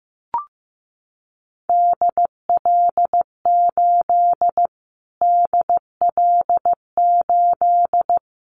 Morse code, heard as a keyed tone, about 15 wpm, 700 Hz. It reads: DL8 DL8
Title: Callsign Prefixes for Top 25 DXCC - Repeated 3x - 15wpm